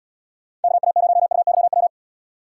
Morse code, heard as a keyed tone, about 78 wpm, 700 Hz. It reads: KA0UYW